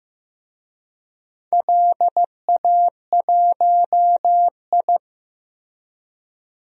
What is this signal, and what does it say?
Morse code, heard as a keyed tone, about 15 wpm, 700 Hz. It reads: LA1I
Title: DXCC-100 Call Signs - 4 Characters - Repeated 3x - 15wpm